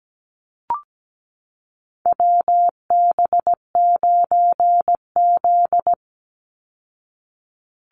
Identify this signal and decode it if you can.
Morse code, heard as a keyed tone, about 17 wpm, 700 Hz. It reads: WB9Z
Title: Call Signs Repeated 2x - 17wpm